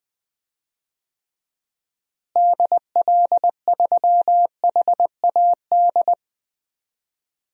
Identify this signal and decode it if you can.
Morse code, heard as a keyed tone, about 20 wpm, 700 Hz. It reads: DL3HAD
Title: DXCC-100 Call Signs - 6 Characters - Repeated 2x - 20wpm